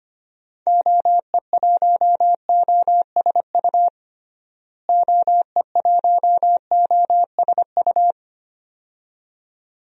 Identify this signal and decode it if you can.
Morse code, heard as a keyed tone, about 25 wpm, 700 Hz. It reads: OE1OSU OE1OSU
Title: DXCC-100 Call Signs - 6 Characters - Repeated 2x - 25wpm